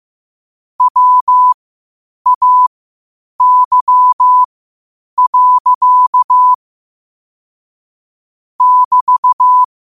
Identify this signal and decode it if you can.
Morse code, heard as a keyed tone, about 15 wpm, spaced as for 8 wpm, 1000 Hz. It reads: WAY. <BT>
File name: qotdfiles / 8wpm